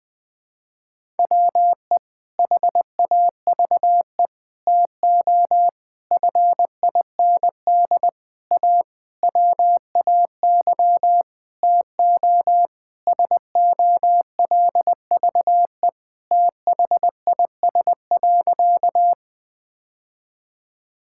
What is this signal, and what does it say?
Morse code, heard as a keyed tone, about 19 wpm, 700 Hz. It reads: WE HAVE TO FIND A WAY TO SOLVE THIS.
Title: Sentences with 6-Word Phrases - 20wpm